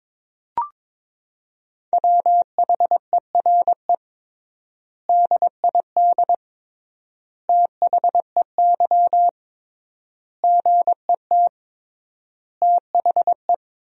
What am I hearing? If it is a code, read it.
Morse code, heard as a keyed tone, about 22 wpm, 700 Hz. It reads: WHERE DID THEY GET THE
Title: 3x Word-Spacing - Sentences from Top 400 Words - 22wpm